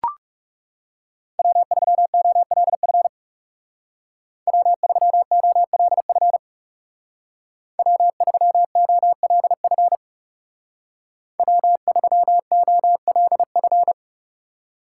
Morse code, heard as a keyed tone, about 37 wpm, 700 Hz. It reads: W3OLF W3OLF W3OLF W3OLF
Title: Speed Racing - USA 1x3 Call Signs 30wpm